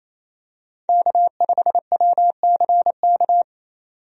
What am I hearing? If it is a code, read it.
Morse code, heard as a keyed tone, about 28 wpm, 700 Hz. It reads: K5WCK
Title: USA Call Signs - 5 Characters - 28wpm